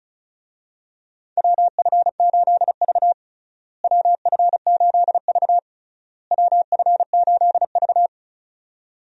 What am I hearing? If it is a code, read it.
Morse code, heard as a keyed tone, about 36 wpm, 700 Hz. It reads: WF8V WF8V WF8V
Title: USA Call Signs - 4 Characters - Repeated 3x - 35wpm